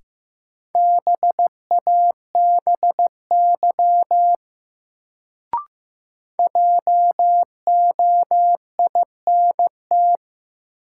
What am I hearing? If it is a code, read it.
Morse code, heard as a keyed tone, about 15 wpm, 700 Hz. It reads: BABY JOINT
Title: Sentences from Top 1300 Words - New Word Drill Alt - 15wpm